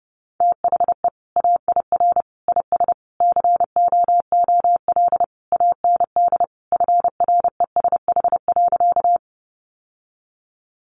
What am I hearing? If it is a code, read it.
Morse code, heard as a keyed tone, about 30 wpm, 700 Hz. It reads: THE AIR IS COOL AND FRESH.
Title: Sentences from Top 1200 Words 30wpm